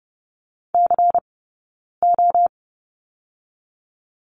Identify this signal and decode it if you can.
Morse code, heard as a keyed tone, about 30 wpm, 700 Hz. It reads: C O
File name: 092 Lesson 16d - Letter c - Two Characters 30wpm